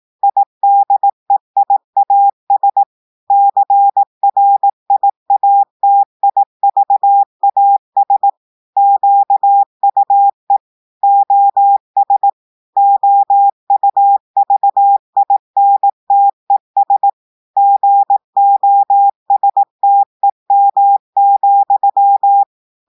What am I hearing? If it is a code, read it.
Morse code, heard as a keyed tone, about 18 wpm, 800 Hz. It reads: IDEIAS CRIATIVAS QUE OS OUVINTES GOSTEM,
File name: audios / portugais / portugais texte en clair 2 vitesse 18 mots minutes